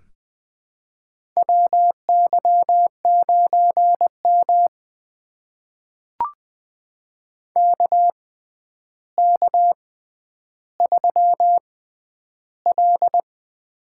Morse code, heard as a keyed tone, about 21 wpm, spaced as for 10 wpm, 700 Hz. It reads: WY9M KK3L
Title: USA Call Signs - 4 Characters - 6x Character Spacing - 20wpm